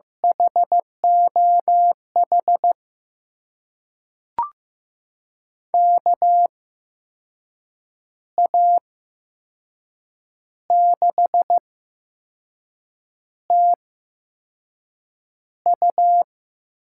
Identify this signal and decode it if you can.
Morse code, heard as a keyed tone, about 15 wpm, spaced as for 5 wpm, 700 Hz. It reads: HOH KA6TU
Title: USA Call Signs - 6 Characters - 8x Character Spacing - 15wpm